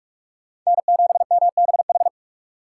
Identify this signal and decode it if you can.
Morse code, heard as a keyed tone, about 46 wpm, 700 Hz. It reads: N7MBH
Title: USA Call Signs - 5 Characters - 4x Character Spacing - 45wpm